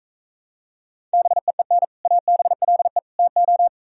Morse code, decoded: BEEN ABLE TO

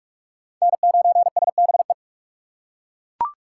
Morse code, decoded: N9SBE

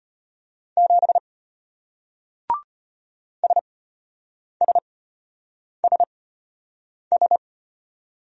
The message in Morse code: 7 SSSS

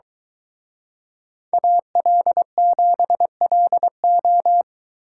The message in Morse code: AL7LO

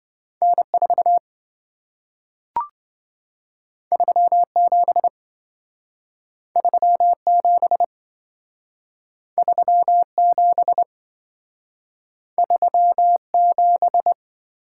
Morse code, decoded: N4 37 37 37 37

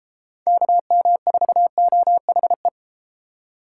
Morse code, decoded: KM4OHE

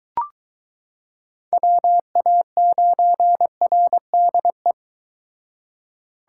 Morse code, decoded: WA9RDE

